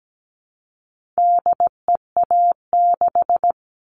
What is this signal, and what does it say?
Morse code, heard as a keyed tone, about 17 wpm, 700 Hz. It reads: DEA6
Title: All Cut Numbers - 4-Digit 17wpm